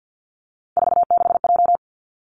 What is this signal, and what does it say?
Morse code, heard as a keyed tone, about 50 wpm, 700 Hz. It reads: 46J
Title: Three Challenging Letter-Number Combinations 50wpm